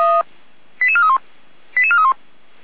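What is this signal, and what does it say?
Five-tone selective call (ZVEI), heard as quick runs of five tones, tones of about 70 ms. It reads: 88431 88431
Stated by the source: ZVEI five-tone selective call received via SDR.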